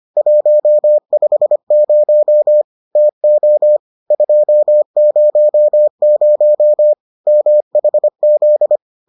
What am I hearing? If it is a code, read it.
Morse code, heard as a keyed tone, about 25 wpm, 600 Hz. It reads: 150 TO 200 MHZ